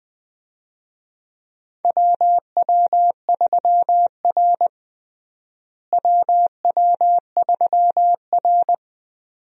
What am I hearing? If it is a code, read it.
Morse code, heard as a keyed tone, about 21 wpm, 700 Hz. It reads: WW3R WW3R
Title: USA Call Signs - 4 Characters - Repeated 2x - 20wpm